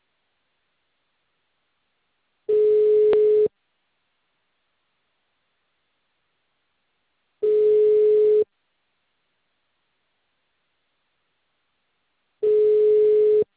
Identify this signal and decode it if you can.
Ringback tone (Europe, 425 Hz).
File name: out- -2030